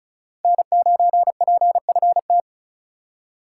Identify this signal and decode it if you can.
Morse code, heard as a keyed tone, about 36 wpm, 700 Hz. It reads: N9PFT
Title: USA Call Signs - 5 Characters - 3x Character Spacing - 35wpm